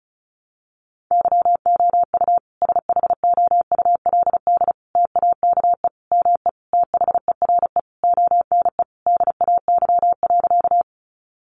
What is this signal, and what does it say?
Morse code, heard as a keyed tone, about 34 wpm, 700 Hz. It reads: YOU SHOULD TAKE ME THERE ONE DAY.
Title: Sentences from Top 300 Words 35wpm